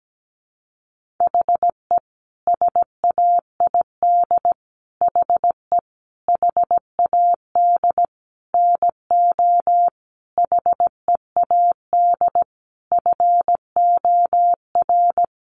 Morse code, heard as a keyed tone, about 17 wpm, 700 Hz. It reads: HE SAID HE HAD NO HEAD FOR
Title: Sentences from Top 1500 Words 17wpm